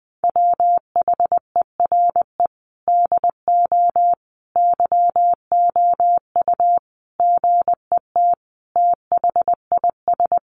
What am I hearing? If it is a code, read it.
Morse code, heard as a keyed tone, about 20 wpm, 700 Hz. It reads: WHERE DO YOU GET THIS
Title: Sentences from Top 1400 Words 20wpm